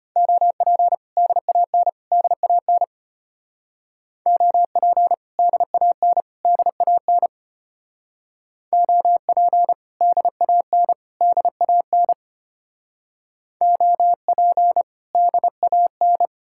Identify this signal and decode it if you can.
Morse code, heard as a keyed tone, about 32 wpm, 700 Hz. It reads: OP DAN DAN OP DAN DAN OP DAN DAN OP DAN